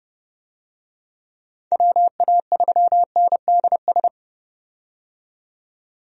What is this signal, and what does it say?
Morse code, heard as a keyed tone, about 30 wpm, 700 Hz. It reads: WA3NDS
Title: USA Call Signs - 6 Characters - 30wpm